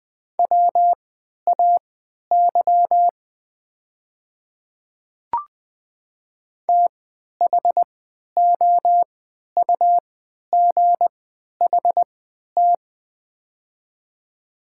Morse code, heard as a keyed tone, about 20 wpm, spaced as for 11 wpm, 700 Hz. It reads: WAY THOUGHT
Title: 3x Character Spacing - Top 300 Words - 20wpm